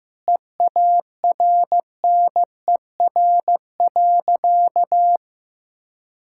Morse code, heard as a keyed tone, about 15 wpm, 700 Hz. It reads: EARNER.
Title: Pangram Challenge v2 - 15wpm